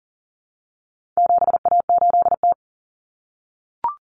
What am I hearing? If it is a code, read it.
Morse code, heard as a keyed tone, about 40 wpm, 700 Hz. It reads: 7A8T
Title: Common Cut Numbers - 4-Digit 40wpm